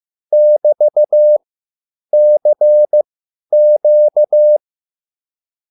Morse code, heard as a keyed tone, about 15 wpm, spaced as for 9 wpm, 600 Hz. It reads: <BT>CQ
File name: QSL / CW lesson